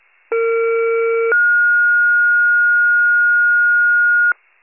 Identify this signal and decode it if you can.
Two-tone page, heard as a one-second tone, then a three-second tone, about 460 then 1430 Hz. It is heard from 0.3 to 4.3 s.
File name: pager / 21 FIRE / wcalert